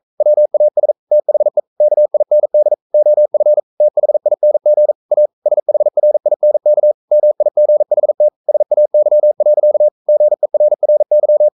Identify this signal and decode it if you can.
Morse code, heard as a keyed tone, about 42 wpm, 600 Hz. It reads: WAS THE KIND OF THING A SHRINK MIGHT SAY. GERRY
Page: Romanen Never af Ken Follet på engelsk.